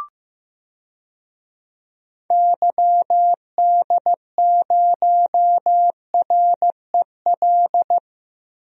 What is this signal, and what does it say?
Morse code, heard as a keyed tone, about 15 wpm, 700 Hz. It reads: YD0REL
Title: DXCC-100 Call Signs - 6 Characters - Repeated 2x - 15wpm